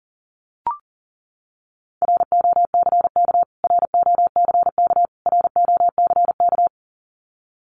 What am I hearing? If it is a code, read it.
Morse code, heard as a keyed tone, about 40 wpm, 700 Hz. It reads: ROCK ROCK ROCK
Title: Top 500 Words Repeated 40wpm